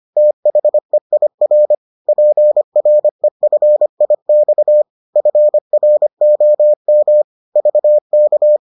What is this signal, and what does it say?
Morse code, heard as a keyed tone, about 25 wpm, 600 Hz. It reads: THEIR PREFIX FROM VK